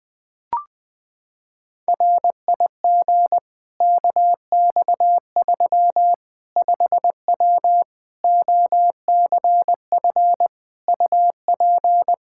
Morse code, heard as a keyed tone, about 21 wpm, 700 Hz. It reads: RIG KX3 5W OCF UP
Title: Practice QSO - Exchange 2 - Info - 20wpm